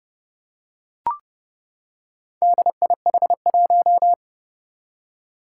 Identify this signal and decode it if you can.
Morse code, heard as a keyed tone, about 29 wpm, 700 Hz. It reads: DIH1
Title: Four Challenging Letter-Number Combinations 30wpm